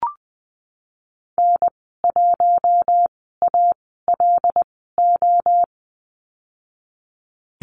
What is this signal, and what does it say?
Morse code, heard as a keyed tone, about 20 wpm, spaced as for 15 wpm, 700 Hz. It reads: N1ALO
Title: 076 Lesson 13g - Letter L - Call Signs 20wpm